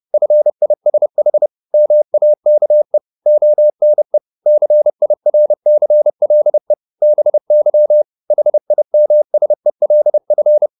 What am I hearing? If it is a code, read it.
Morse code, heard as a keyed tone, about 30 wpm, 600 Hz. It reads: FISH MAKE ONE CIRCLE BY HIMSELF